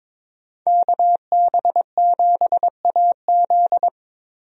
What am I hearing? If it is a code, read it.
Morse code, heard as a keyed tone, about 21 wpm, 700 Hz. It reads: KB7AZ
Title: USA Call Signs - 5 Characters - Repeated 3x - 22wpm